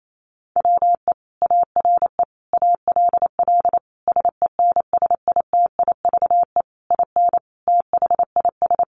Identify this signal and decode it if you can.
Morse code, heard as a keyed tone, about 28 wpm, 700 Hz. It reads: WE ARE ALL SENSITIVE IN THIS
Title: Sentences from Top 1400 Words 28wpm